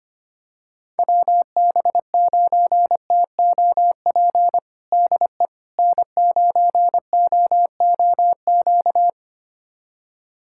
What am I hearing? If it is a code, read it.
Morse code, heard as a keyed tone, about 25 wpm, 700 Hz. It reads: WB9TOP DE N9OOQ
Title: USA Callsign DE Callsign 25wpm